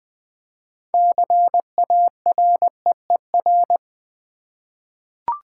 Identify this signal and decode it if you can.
Morse code, heard as a keyed tone, about 20 wpm, 700 Hz. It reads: CAREER
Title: Six Letter Words - 20wpm